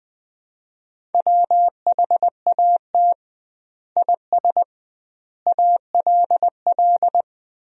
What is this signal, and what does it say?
Morse code, heard as a keyed tone, about 20 wpm, 700 Hz. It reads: WHAT IS ALL
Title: 2x Word-Spacing - Sentences from Top 100 Words - 20wpm